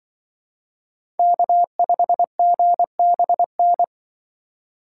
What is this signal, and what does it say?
Morse code, heard as a keyed tone, about 24 wpm, 700 Hz. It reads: K5GBN
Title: Speed Racing - USA 1x3 Call Signs 20wpm